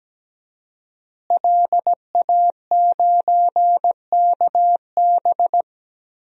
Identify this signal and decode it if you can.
Morse code, heard as a keyed tone, about 17 wpm, 700 Hz. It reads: LA9KB
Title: DXCC-100 Call Signs - 5 Characters - Repeated 3x - 17wpm